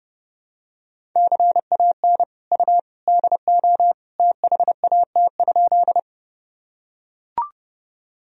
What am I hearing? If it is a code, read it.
Morse code, heard as a keyed tone, about 30 wpm, 700 Hz. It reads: CAN U DO THAT?
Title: Sentences from Top 100 Words and Abbreviations 30wpm